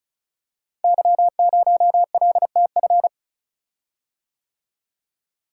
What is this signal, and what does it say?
Morse code, heard as a keyed tone, about 36 wpm, 700 Hz. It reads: Y0LTF
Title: Code Groups - 5 Letter:Numbers - 35wpm